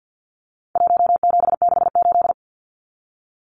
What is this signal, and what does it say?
Morse code, heard as a keyed tone, about 48 wpm, 700 Hz. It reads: 1768
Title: Numbers - Contest Style - 4-Digit 50wpm